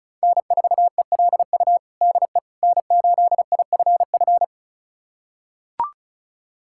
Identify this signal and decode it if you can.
Morse code, heard as a keyed tone, about 36 wpm, 700 Hz. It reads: N4ELU DE N8IFF
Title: USA Callsign DE Callsign - 5 Characters - 35wpm